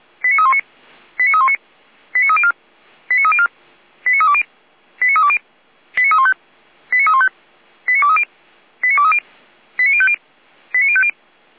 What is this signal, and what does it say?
Five-tone selective call (ZVEI), heard as quick runs of five tones, tones of about 70 ms. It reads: 89319 89319 89304 89304 89320 89320 89315 89315 89230 89230 89050 89050